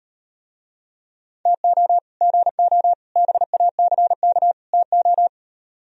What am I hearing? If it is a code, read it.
Morse code, heard as a keyed tone, about 37 wpm, 700 Hz. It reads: TO GO BACK TO